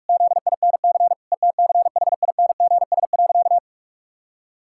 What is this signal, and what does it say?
Morse code, heard as a keyed tone, about 46 wpm, 700 Hz. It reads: ZINC ETCHINGS.
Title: Pangram Challenge v2 - 45wpm